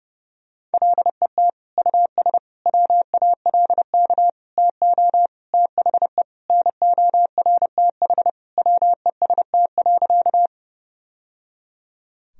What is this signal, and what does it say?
Morse code, heard as a keyed tone, about 30 wpm, 700 Hz. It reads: LET US WALK TO THE NORTH WEST.
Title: Sentences from Top 1000 Words - Review Part 2 - 30wpm